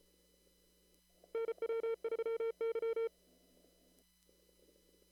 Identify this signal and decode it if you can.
Morse code, heard as a keyed tone, about 31 wpm, 500 Hz. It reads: NW3Y
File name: CW / kt5j wpx cw-op-k5pi